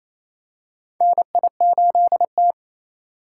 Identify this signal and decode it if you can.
Morse code, heard as a keyed tone, about 28 wpm, 700 Hz. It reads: NI8T
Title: USA Call Signs - 4 Characters - 6x Character Spacing - 28wpm